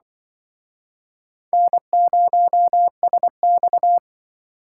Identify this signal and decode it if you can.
Morse code, heard as a keyed tone, about 26 wpm, 700 Hz. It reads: N0SX